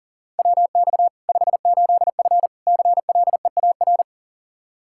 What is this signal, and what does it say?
Morse code, heard as a keyed tone, about 40 wpm, 700 Hz. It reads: WX 58F CLEAR
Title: Practice QSO - Exchange 2 - Info - 40wpm